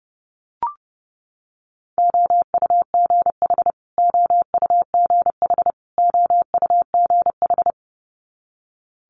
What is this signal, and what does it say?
Morse code, heard as a keyed tone, about 30 wpm, 700 Hz. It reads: OUGH OUGH OUGH
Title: Multi-letter Phonemes Repeated 3x 30wpm